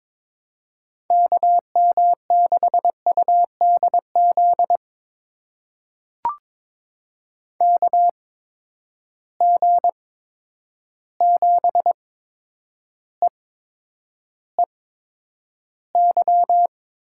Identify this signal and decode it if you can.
Morse code, heard as a keyed tone, about 22 wpm, spaced as for 9 wpm, 700 Hz. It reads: KM6UDZ KG7EEY
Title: USA Call Signs - 6 Characters - 8x Character Spacing - 22wpm